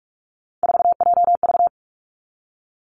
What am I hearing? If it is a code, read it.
Morse code, heard as a keyed tone, about 46 wpm, 700 Hz. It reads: 4JV
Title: Three Challenging Letter-Number Combinations 45wpm